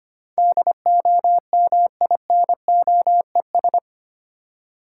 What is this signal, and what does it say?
Morse code, heard as a keyed tone, about 29 wpm, 700 Hz. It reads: DOMINOES
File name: Easy Words - Speed-Racing with Hints - 25wpm